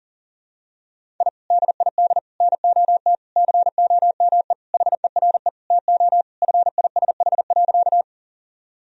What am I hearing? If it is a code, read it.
Morse code, heard as a keyed tone, about 40 wpm, 700 Hz. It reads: I DID NOT COME HERE TO FISH.
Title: Sentences from Top 1000 Words - Review Part 1 - 40wpm